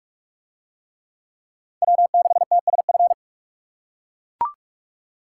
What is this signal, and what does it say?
Morse code, heard as a keyed tone, about 40 wpm, 700 Hz. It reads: W6TSF